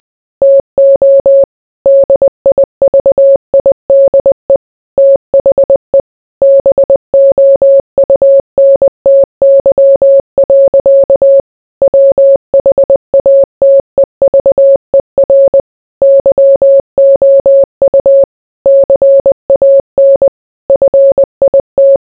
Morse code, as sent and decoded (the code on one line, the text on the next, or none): - --- / -.. .. ...- .. -.. / - .... / -... --- ..- -. - -.-- .-.-.- / .-- .... .- - . ...- . .-. / -.-- --- ..- / -.-. .- -. / ..-. .. -
TO DIVIDE THE BOUNTY. WHATEVER YOU CAN FIT